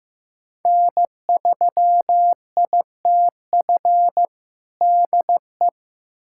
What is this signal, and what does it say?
Morse code, heard as a keyed tone, about 16 wpm, 700 Hz. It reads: N3ITF DE
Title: USA Callsign DE Callsign - 5 Characters - 15wpm